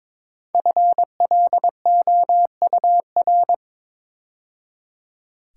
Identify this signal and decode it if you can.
Morse code, heard as a keyed tone, about 22 wpm, 690 Hz. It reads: FLOUR
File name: 5 Letter Words - First 3 Letters Given - 22wpm